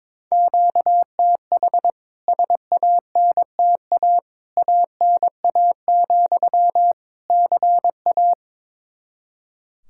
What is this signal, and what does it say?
Morse code, heard as a keyed tone, about 22 wpm, 700 Hz. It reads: QTH SANTA ANA, CA
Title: K1USN SST QSO Practice 22wpm